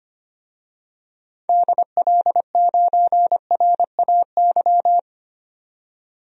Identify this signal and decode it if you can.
Morse code, heard as a keyed tone, about 25 wpm, 700 Hz. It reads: DL9RAY
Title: DXCC-100 Call Signs - 6 Characters - 25wpm